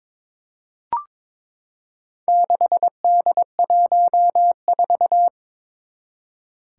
Morse code, heard as a keyed tone, about 22 wpm, 700 Hz. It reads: 6D14
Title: Four Challenging Letter-Number Combinations 22wpm